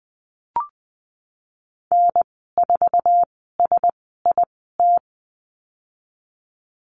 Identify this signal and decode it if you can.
Morse code, heard as a keyed tone, about 20 wpm, spaced as for 15 wpm, 700 Hz. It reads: N4SIT
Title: 088 Lesson 15g - Number 5 - Call Signs 20wpm